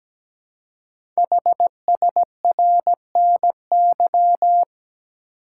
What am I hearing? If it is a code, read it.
Morse code, heard as a keyed tone, about 17 wpm, 700 Hz. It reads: HSRNY
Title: Code Groups - 5 Letters - 17wpm